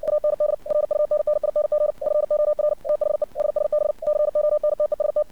Morse code, heard as a keyed tone, about 57 wpm, 600 Hz. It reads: AND RUNNING FOR THE SSB CONTEST
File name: archived / CRSnet